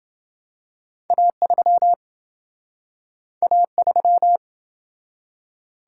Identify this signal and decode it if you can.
Morse code, heard as a keyed tone, about 27 wpm, 700 Hz. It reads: A3 A3